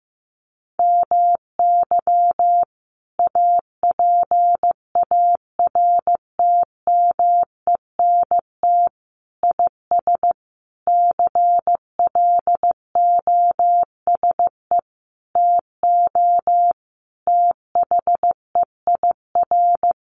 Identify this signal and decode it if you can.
Morse code, heard as a keyed tone, about 15 wpm, 700 Hz. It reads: MY APARTMENT IS CLOSE TO THEIR
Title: Sentences from Top 900 Words 15wpm